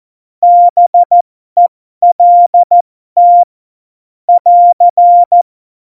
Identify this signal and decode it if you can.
Morse code, heard as a keyed tone, about 14 wpm, spaced as for 12 wpm, 700 Hz. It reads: BELT <AR>
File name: Morsekurs / Witze